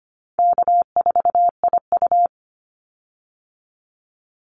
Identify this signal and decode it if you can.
Morse code, heard as a keyed tone, about 25 wpm, 700 Hz. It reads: K4IU
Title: Call Signs - 25wpm